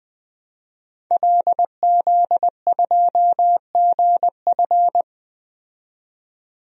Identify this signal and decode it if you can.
Morse code, heard as a keyed tone, about 20 wpm, 700 Hz. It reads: LZ2GF
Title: DXCC-100 Call Signs - 5 Characters - 20wpm